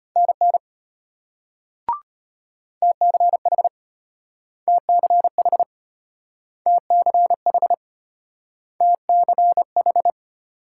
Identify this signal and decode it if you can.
Morse code, heard as a keyed tone, about 33 wpm, 700 Hz. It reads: NN TCH TCH TCH TCH